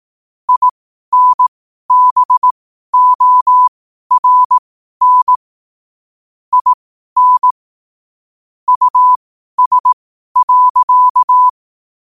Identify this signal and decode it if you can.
Morse code, heard as a keyed tone, about 18 wpm, spaced as for 12 wpm, 1000 Hz. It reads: INBORN IN US.
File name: qotdfiles / 12wpm